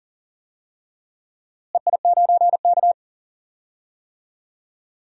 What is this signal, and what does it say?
Morse code, heard as a keyed tone, about 40 wpm, 700 Hz. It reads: EI9K
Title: DXCC-100 Call Signs - 4 Characters - Repeated 2x - 40wpm